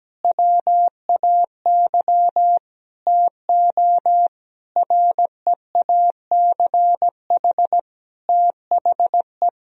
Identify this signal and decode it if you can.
Morse code, heard as a keyed tone, about 17 wpm, 700 Hz. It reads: WAY TO REACH THE
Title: Sentences from Top 1000 Words - Review Part 2 - 17wpm